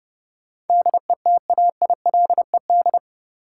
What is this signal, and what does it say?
Morse code, heard as a keyed tone, about 31 wpm, 700 Hz. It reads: DETAILED
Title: Sentences from Top 1500 Words - New Word Drill - 30wpm